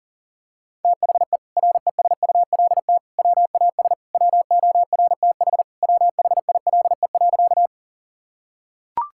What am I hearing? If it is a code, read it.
Morse code, heard as a keyed tone, about 40 wpm, 700 Hz. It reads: THE RESULT WAS WORTH WHILE.
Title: Sentences from Top 1000 Words - Review Part 5 - 40wpm